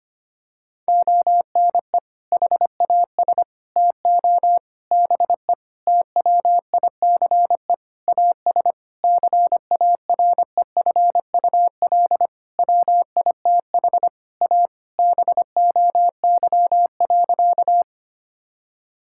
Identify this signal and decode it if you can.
Morse code, heard as a keyed tone, about 25 wpm, 700 Hz. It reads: ONE HAS TO BE TWICE AS CAREFUL WITH A BOY.
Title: Sentences from Top 1600 Words 25wpm